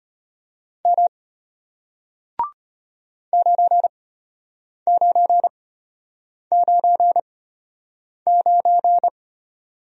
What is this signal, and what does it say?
Morse code, heard as a keyed tone, about 33 wpm, spaced as for 8 wpm, 700 Hz. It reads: M 9999